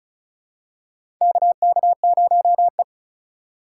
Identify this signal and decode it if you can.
Morse code, heard as a keyed tone, about 36 wpm, 700 Hz. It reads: KK0E
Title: USA Call Signs - 4 Characters - 2x Character Spacing - 35wpm